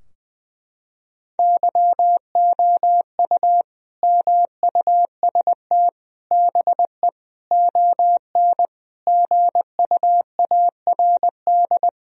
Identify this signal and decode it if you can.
Morse code, heard as a keyed tone, about 20 wpm, 700 Hz. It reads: YOU MUST BE ON GUARD
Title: Sentences from Top 1000 Words - Review Part 4 - 20wpm